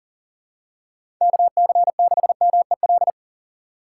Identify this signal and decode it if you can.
Morse code, heard as a keyed tone, about 40 wpm, 700 Hz. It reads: KC6MEL
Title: USA Call Signs - 6 Characters - 4x Character Spacing - 40wpm